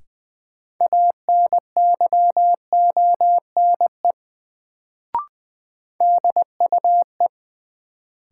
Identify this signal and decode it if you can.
Morse code, heard as a keyed tone, about 20 wpm, 700 Hz. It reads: ANYONE DUE
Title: Sets of 5 Words Encore - New Word Drill - 20wpm